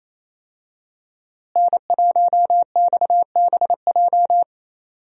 Morse code, heard as a keyed tone, about 27 wpm, 700 Hz. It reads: N1XBJ
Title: USA Call Signs - 5 Characters - Repeated 2x - 28wpm